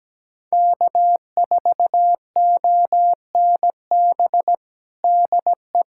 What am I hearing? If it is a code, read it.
Morse code, heard as a keyed tone, about 17 wpm, 700 Hz. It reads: K4ONB DE
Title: USA Callsign DE Callsign 17wpm